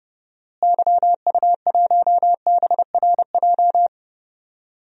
Morse code, heard as a keyed tone, about 32 wpm, 700 Hz. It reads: YU1BRJ
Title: DXCC-100 Call Signs - 6 Characters - Repeated 2x - 30wpm